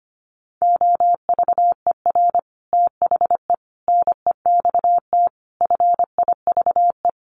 Morse code, heard as a keyed tone, about 25 wpm, 700 Hz. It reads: OVER THE NEXT FIVE
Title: Sets of 5 Words 25wpm